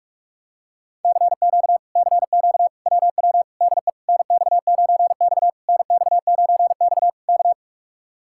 Morse code, heard as a keyed tone, about 48 wpm, 700 Hz. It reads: CQ CQ WW DE NX9X NX9X K